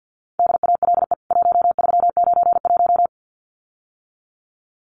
Morse code, heard as a keyed tone, about 50 wpm, 700 Hz. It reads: DALE 1391
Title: CWT Exchange 50wpm